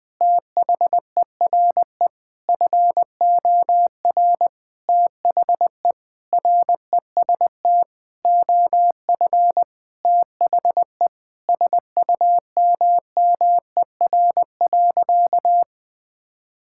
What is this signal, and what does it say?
Morse code, heard as a keyed tone, about 21 wpm, 700 Hz. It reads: THERE FOR THE REST OF THE SUMMER.
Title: Sentences with 5-Word Phrases - 20wpm